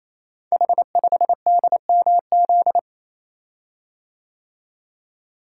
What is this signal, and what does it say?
Morse code, heard as a keyed tone, about 27 wpm, 700 Hz. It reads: H5DMZ
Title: Code Groups - 5 Letter:Numbers - 28wpm